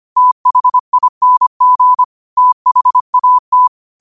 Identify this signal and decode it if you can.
Morse code, heard as a keyed tone, about 25 wpm, 1000 Hz. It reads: THING THAT
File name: qotdfiles / 25wpm